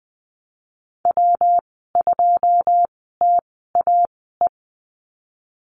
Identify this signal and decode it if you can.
Morse code, heard as a keyed tone, about 20 wpm, spaced as for 15 wpm, 700 Hz. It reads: W2TAE
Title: 161 Lesson 26g - Letter g - Call Signs 20wpm